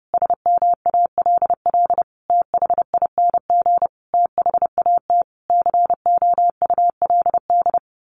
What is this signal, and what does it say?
Morse code, heard as a keyed tone, about 30 wpm, 700 Hz. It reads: SMALL THING THAT COULD
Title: A Sound Of Thunder 30wpm